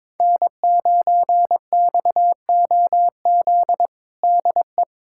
Morse code, heard as a keyed tone, about 22 wpm, 700 Hz. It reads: N9XOZ DE
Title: USA Callsign DE Callsign - 5 Characters - 22wpm